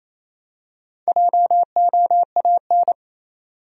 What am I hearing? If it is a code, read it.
Morse code, heard as a keyed tone, about 28 wpm, 700 Hz. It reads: JOAN
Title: USA Common Names - Max 5 Letters - 28wpm